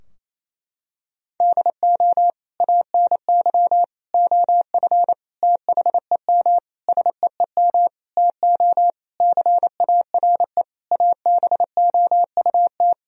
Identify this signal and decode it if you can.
Morse code, heard as a keyed tone, about 28 wpm, 700 Hz. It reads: DO ANY OF THEM SEEM TO CARE ABOUT
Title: Sentences from Top 1000 Words - Review Part 3 - 28wpm